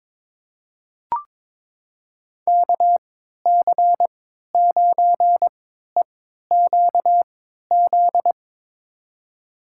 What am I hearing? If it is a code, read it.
Morse code, heard as a keyed tone, about 23 wpm, spaced as for 13 wpm, 700 Hz. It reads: KC9EQZ
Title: USA Call Signs - 3x Character Spacing - 22wpm